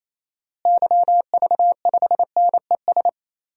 Morse code, heard as a keyed tone, about 28 wpm, 700 Hz. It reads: YV5NES
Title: DXCC-100 Call Signs - 6 Characters - Repeated 2x - 28wpm